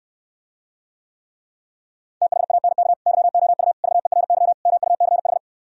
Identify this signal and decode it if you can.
Morse code, heard as a keyed tone, about 68 wpm, 700 Hz. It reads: THANK YOU VERY MUCH